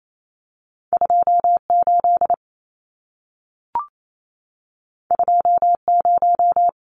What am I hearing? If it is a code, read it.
Morse code, heard as a keyed tone, about 28 wpm, 700 Hz. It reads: 28 20
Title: Numbers - 2-Digit 28wpm